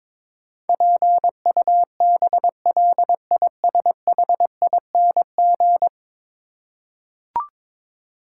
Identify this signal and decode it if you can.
Morse code, heard as a keyed tone, about 22 wpm, 700 Hz. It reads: PUBLISHING
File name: practice / 22wpm / 10 Letter Words - First 3 Letters Given - 22wpm